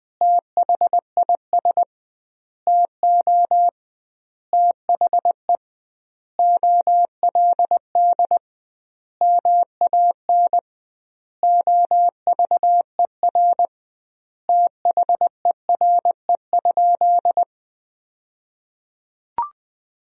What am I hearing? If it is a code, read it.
Morse code, heard as a keyed tone, about 21 wpm, 700 Hz. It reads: THIS TO THE OLD MAN OVER THERE?
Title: 2x Word-Spacing - Sentences from Top 200 Words - 20wpm